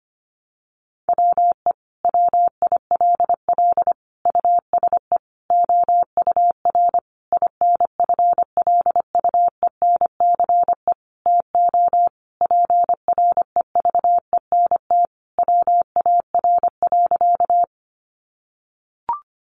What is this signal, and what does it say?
Morse code, heard as a keyed tone, about 25 wpm, 700 Hz. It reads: WE WILL USE OUR INFLUENCE TO PREVENT WAR.
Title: Sentences from Top 1100 Words 25wpm